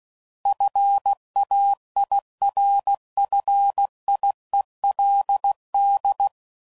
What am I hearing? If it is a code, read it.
Morse code, heard as a keyed tone, about 16 wpm, 800 Hz. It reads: FAIRFIELD
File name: CW Code Groups @15WPM